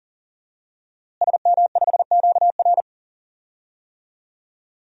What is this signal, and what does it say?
Morse code, heard as a keyed tone, about 40 wpm, 700 Hz. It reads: SM5QR
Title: DXCC-100 Call Signs - 5 Characters - Repeated 3x - 40wpm